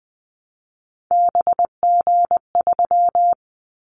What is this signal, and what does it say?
Morse code, heard as a keyed tone, about 20 wpm, 700 Hz. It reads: BG3
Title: Callsign Prefixes for Top 100 DXCC - 20wpm